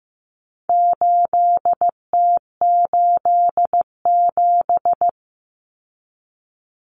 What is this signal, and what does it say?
Morse code, heard as a keyed tone, about 15 wpm, 700 Hz. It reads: 8T87
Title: Common Cut Numbers - 4-Digit 15wpm